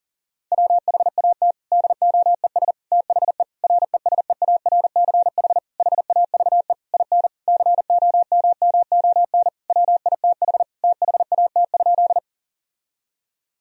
Morse code, heard as a keyed tone, about 40 wpm, 700 Hz. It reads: WHAT DOES THE RESEARCH HAVE IN COMMON WITH THAT?
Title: Sentences from Top 1000 Words - Review Part 4 - 40wpm